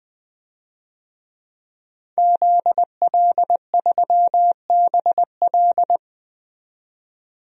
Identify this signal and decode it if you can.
Morse code, heard as a keyed tone, about 20 wpm, 700 Hz. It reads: ZL3BL
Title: DXCC-100 Call Signs - 5 Characters - 20wpm